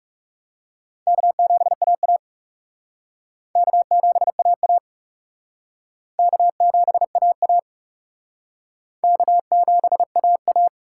Morse code, heard as a keyed tone, about 37 wpm, 700 Hz. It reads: K7AA K7AA K7AA K7AA
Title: Speed Racing - USA Call Signs 30wpm